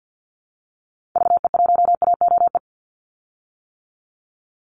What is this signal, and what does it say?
Morse code, heard as a keyed tone, about 50 wpm, 700 Hz. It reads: VE1AOE
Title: Call Signs - 50wpm